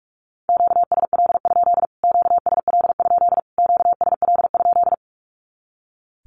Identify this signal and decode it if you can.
Morse code, heard as a keyed tone, about 44 wpm, 700 Hz. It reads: QSL? QSL? QSL?
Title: Q-Codes Repeated 3x 45wpm